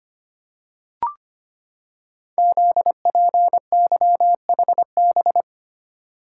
Morse code, heard as a keyed tone, about 25 wpm, 700 Hz. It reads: ZPYHB
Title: Code Groups - 5 Letter:Numbers - 25wpm